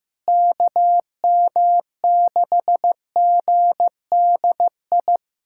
Morse code, heard as a keyed tone, about 15 wpm, 700 Hz. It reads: KM6GDI
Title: USA Callsign DE Callsign - 6 Characters - 15wpm